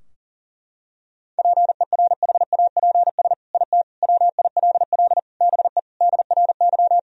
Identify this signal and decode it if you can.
Morse code, heard as a keyed tone, about 40 wpm, 700 Hz. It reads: PERHAPS IT WILL BE DRY
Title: Sentences from Top 1000 Words - Review Part 3 - 40wpm